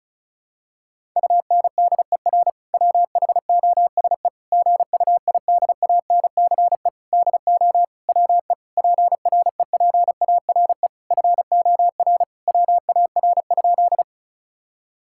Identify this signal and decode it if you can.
Morse code, heard as a keyed tone, about 36 wpm, 700 Hz. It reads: UNDER WHOSE GUIDANCE DO WE PREPARE FOR WAR?
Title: Sentences from Top 1600 Words 35wpm